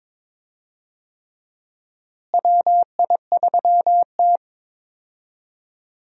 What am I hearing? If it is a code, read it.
Morse code, heard as a keyed tone, about 21 wpm, 700 Hz. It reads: WI3T
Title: USA Call Signs - 4 Characters - Repeated 3x - 22wpm